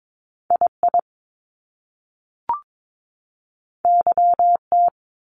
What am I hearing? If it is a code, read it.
Morse code, heard as a keyed tone, about 22 wpm, 700 Hz. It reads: II YT
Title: Top Two Letter Combinations in Words 22wpm